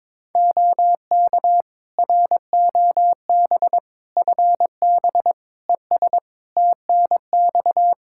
Morse code, heard as a keyed tone, about 22 wpm, 700 Hz. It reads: OK ROB FB ES TNX
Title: Practice QSO - Exchange 2 - Info - 22wpm